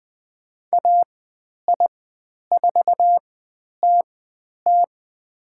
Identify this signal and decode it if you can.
Morse code, heard as a keyed tone, about 20 wpm, spaced as for 10 wpm, 700 Hz. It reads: AI4TT
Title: 102 Lesson 17g - Letter u - Call Signs 20wpm